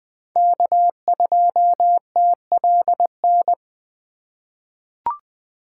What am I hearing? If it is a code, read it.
Morse code, heard as a keyed tone, about 20 wpm, 700 Hz. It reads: K2TLN
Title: USA Call Signs - 2x Character Spacing - 20wpm